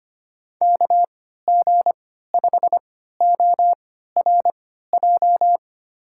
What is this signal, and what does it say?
Morse code, heard as a keyed tone, about 25 wpm, spaced as for 14 wpm, 700 Hz. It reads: KG5ORJ
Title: USA Call Signs - 6 Characters - 3x Character Spacing - 25wpm